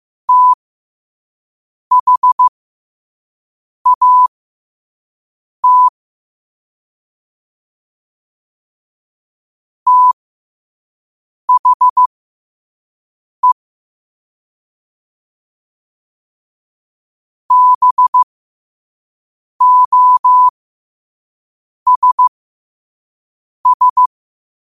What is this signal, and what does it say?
Morse code, heard as a keyed tone, about 15 wpm, spaced as for 5 wpm, 1000 Hz. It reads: THAT THE BOSS
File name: qotdfiles / 5wpm